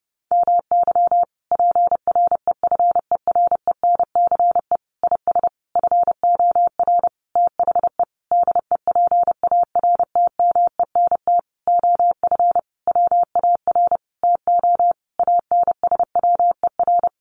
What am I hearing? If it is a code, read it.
Morse code, heard as a keyed tone, about 30 wpm, 700 Hz. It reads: MY PREFERENCE IS FOR THE DEPARTMENT OF WAR TO ANSWER
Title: Sentences from Top 1500 Words 30wpm